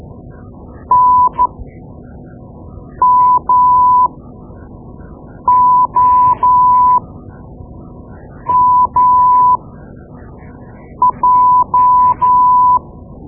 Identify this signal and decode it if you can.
Morse code, heard as a keyed tone, about 13 wpm, spaced as for 5 wpm, 1000 Hz. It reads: NMOMJ
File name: Morse Lessons / Morse Lessons / Tape-2B